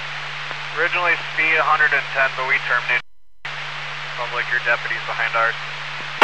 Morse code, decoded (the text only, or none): none